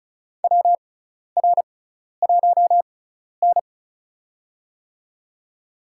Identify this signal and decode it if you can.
Morse code, heard as a keyed tone, about 36 wpm, spaced as for 12 wpm, 700 Hz. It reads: WR1N
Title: USA Call Signs - 4 Characters - 6x Character Spacing - 35wpm